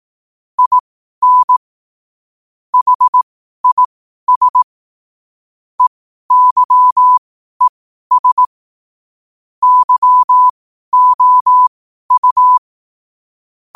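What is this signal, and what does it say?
Morse code, heard as a keyed tone, about 18 wpm, spaced as for 12 wpm, 1000 Hz. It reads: IN HIS EYES YOU
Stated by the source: Morse code sent at 12 words per minute.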